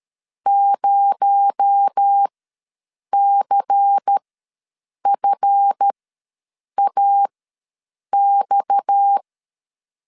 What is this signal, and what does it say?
Morse code, heard as a keyed tone, about 13 wpm, spaced as for 7 wpm, 800 Hz. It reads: 0CFAX
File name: LEZIONI / 7 GF9Z / 7 GF ws 12wpm 3140